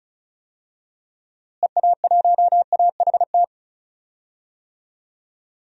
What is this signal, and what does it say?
Morse code, heard as a keyed tone, about 34 wpm, 700 Hz. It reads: EA1AHT
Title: DXCC-100 Call Signs - 6 Characters - Repeated 2x - 35wpm